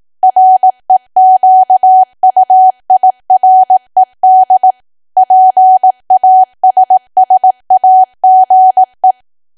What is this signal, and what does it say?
Morse code, heard as a keyed tone, about 18 wpm, 750 Hz. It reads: REQUIRED PASSAGE
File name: cwa / W1AW-Level2 / 18WPM